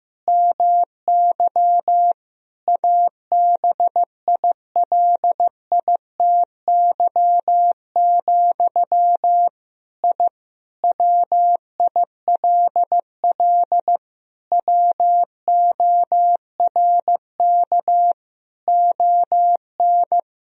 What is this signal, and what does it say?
Morse code, heard as a keyed tone, about 15 wpm, 700 Hz. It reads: MY ABILITY, I WILL WORK ON
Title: Sentences from Top 1000 Words - Review Part 3 - 15wpm